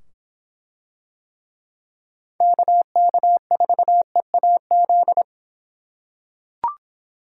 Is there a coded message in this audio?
Yes: Morse code from 2.4 to 5.2 s, heard as a keyed tone, about 23 wpm, 700 Hz. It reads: KK4EAZ